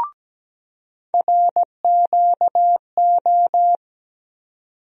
Morse code, heard as a keyed tone, about 17 wpm, 700 Hz. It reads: RQO
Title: USA Call Sign Postfixes - 3 Letters 17wpm